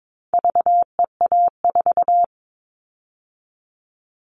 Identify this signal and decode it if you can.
Morse code, heard as a keyed tone, about 22 wpm, 700 Hz. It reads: VEA4
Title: All Cut Numbers - 4-Digit 22wpm